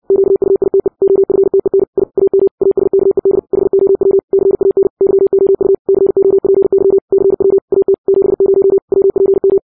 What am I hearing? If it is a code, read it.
Morse code, heard as a keyed tone, about 60 wpm, 390 Hz. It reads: QUITE OFTEN I AM ASKED HOW CAN YOU COPY CW AT 70 WPM
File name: 60wpm